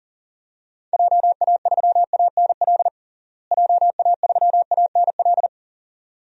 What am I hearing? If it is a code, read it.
Morse code, heard as a keyed tone, about 40 wpm, 700 Hz. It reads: JA3ANL JA3ANL
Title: DXCC-100 Call Signs - 6 Characters - Repeated 2x - 40wpm